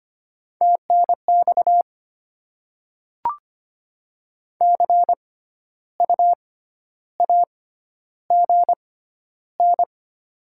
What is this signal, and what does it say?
Morse code, heard as a keyed tone, about 25 wpm, spaced as for 12 wpm, 700 Hz. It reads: TNX CU AGN
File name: Top 20 QSO Elements - 6x Character Spacing - 25wpm